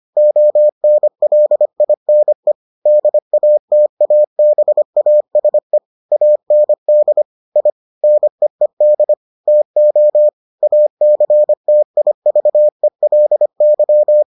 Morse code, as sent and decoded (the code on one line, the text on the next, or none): --- -. .-.. .. -. / -.. .- - .- -... .- ... / .- -. -.. / .. / -. . -.. / - --- / .- -.-. - .. ...- . .-.. -.--
ONLINE DATABASE AND I NEED TO ACTIVELY